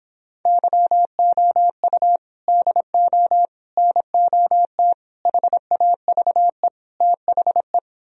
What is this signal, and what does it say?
Morse code, heard as a keyed tone, about 26 wpm, 700 Hz. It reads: YOU DO NOT HAVE THE